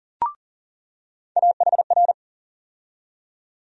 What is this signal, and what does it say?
Morse code, heard as a keyed tone, about 40 wpm, 700 Hz. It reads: AHR
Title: USA Call Sign Postfixes - 3 Letters 40wpm